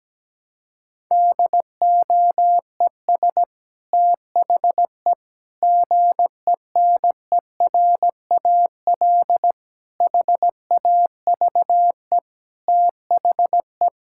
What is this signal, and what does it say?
Morse code, heard as a keyed tone, about 17 wpm, 700 Hz. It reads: DOES THE GENERAL HAVE THE
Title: Sentences from Top 1000 Words - Review Part 6 - 17wpm